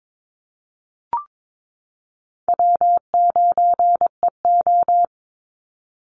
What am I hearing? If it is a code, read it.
Morse code, heard as a keyed tone, about 22 wpm, 700 Hz. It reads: W9EO
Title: Call Signs Repeated 3x - 22wpm